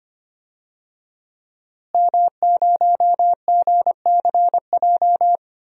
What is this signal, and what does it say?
Morse code, heard as a keyed tone, about 25 wpm, 700 Hz. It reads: M0GCJ
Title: DXCC-100 Call Signs - 5 Characters - Repeated 2x - 25wpm